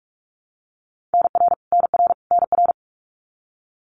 Morse code, heard as a keyed tone, about 46 wpm, 700 Hz. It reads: NR NR NR
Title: Abbreviations Repeated 3x - 45wpm